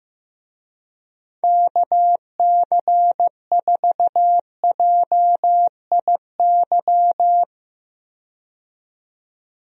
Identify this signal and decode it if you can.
Morse code, heard as a keyed tone, about 15 wpm, 700 Hz. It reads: KC4JIY
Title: USA Call Signs - 6 Characters - 15wpm